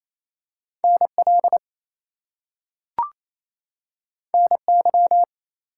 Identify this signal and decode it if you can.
Morse code, heard as a keyed tone, about 28 wpm, 700 Hz. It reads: NL NY
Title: US States and Canadian Provinces 28wpm